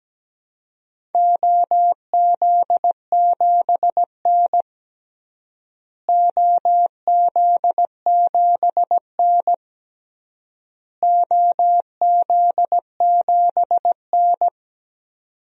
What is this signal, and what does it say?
Morse code, heard as a keyed tone, about 17 wpm, 700 Hz. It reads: OZ7N OZ7N OZ7N
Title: DXCC-100 Call Signs - 4 Characters - Repeated 3x - 17wpm